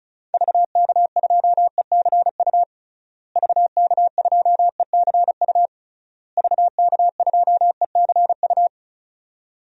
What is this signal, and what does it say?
Morse code, heard as a keyed tone, about 36 wpm, 700 Hz. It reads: VK2ECU VK2ECU VK2ECU
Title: DXCC-100 Call Signs - 6 Characters - Repeated 3x - 35wpm